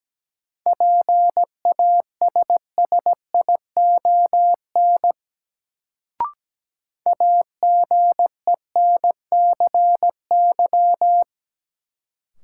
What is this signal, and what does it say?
Morse code, heard as a keyed tone, about 17 wpm, 700 Hz. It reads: PASSION AGENCY
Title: Sentences from Top 1300 Words - New Word Drill Alt - 17wpm